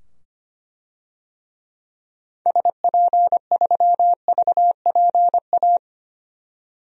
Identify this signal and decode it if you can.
Morse code, heard as a keyed tone, about 25 wpm, 700 Hz. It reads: SP3VPA
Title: DXCC-100 Call Signs - 6 Characters - 25wpm